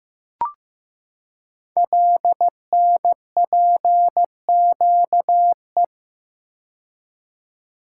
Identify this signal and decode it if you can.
Morse code, heard as a keyed tone, about 15 wpm, 700 Hz. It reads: LNPQE
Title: Code Groups - 5 Letters - 15wpm